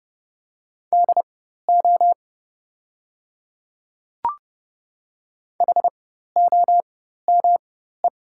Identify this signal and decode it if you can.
Morse code, heard as a keyed tone, about 30 wpm, spaced as for 14 wpm, 700 Hz. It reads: DO HOME
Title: 4x Character Spacing - Top 200 Words - 30wpm